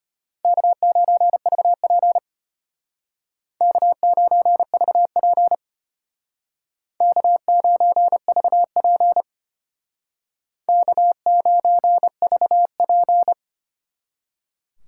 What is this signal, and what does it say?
Morse code, heard as a keyed tone, about 33 wpm, 700 Hz. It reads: K9VP K9VP K9VP K9VP